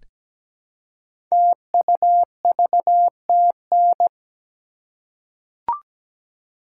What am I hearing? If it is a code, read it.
Morse code, heard as a keyed tone, about 17 wpm, 700 Hz. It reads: TUVTN
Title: Code Groups - 5 Letters - 17wpm